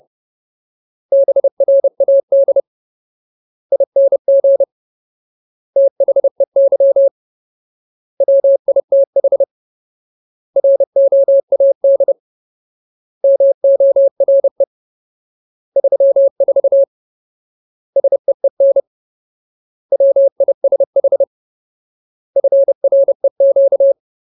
Morse code, heard as a keyed tone, about 30 wpm, 550 Hz. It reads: BRAD ING THEY WITH ROAD MORE 34 SEEN WISH FREQ
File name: Ultimate Word List / 30WPM Ultimate